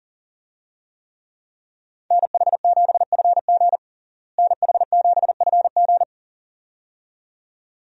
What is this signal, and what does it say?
Morse code, heard as a keyed tone, about 37 wpm, 700 Hz. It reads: NH7FG NH7FG
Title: USA Call Signs - 5 Characters - Repeated 2x - 40wpm